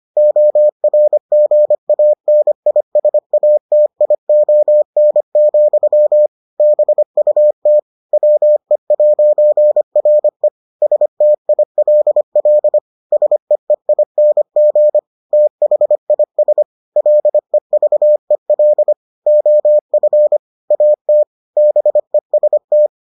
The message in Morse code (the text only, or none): ORGANISATION, BUT WE'RE STILL SEEING THIS LEVEL OF AT BEST